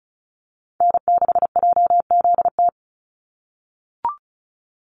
Morse code, heard as a keyed tone, about 34 wpm, 700 Hz. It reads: N6JZT
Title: NA SOTA Activator Callsigns 35wpm